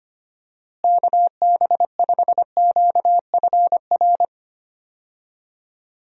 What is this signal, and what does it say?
Morse code, heard as a keyed tone, about 25 wpm, 700 Hz. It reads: KB5QFR
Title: USA Call Signs - 6 Characters - Repeated 3x - 25wpm